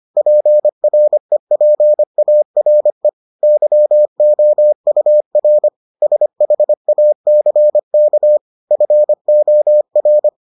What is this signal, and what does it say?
Morse code, heard as a keyed tone, about 25 wpm, 600 Hz. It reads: PREPARE YOUR SHACK FOR